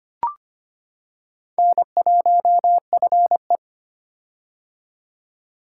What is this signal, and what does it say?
Morse code, heard as a keyed tone, about 25 wpm, 700 Hz. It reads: N1FE
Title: K1USN SST QSO Practice - 25wpm